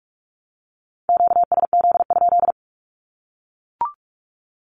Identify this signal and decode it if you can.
Morse code, heard as a keyed tone, about 44 wpm, 700 Hz. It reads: QSZ?
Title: Q-Codes 45wpm